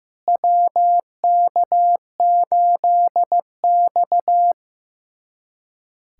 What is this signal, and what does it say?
Morse code, heard as a keyed tone, about 15 wpm, 700 Hz. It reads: WK8X
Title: USA Callsign DE Callsign - 4 Characters - 15wpm